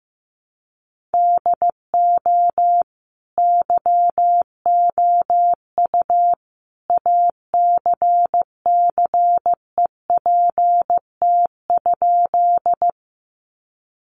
Morse code, heard as a keyed tone, about 15 wpm, 700 Hz. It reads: DO YOU ACCEPT?
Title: Sentences from Top 1300 Words 15wpm